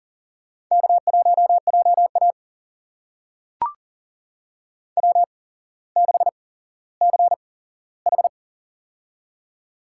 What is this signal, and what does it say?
Morse code, heard as a keyed tone, about 40 wpm, spaced as for 17 wpm, 700 Hz. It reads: K1JA W6CH
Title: USA Call Signs - 4 Characters - 8x Character Spacing - 40wpm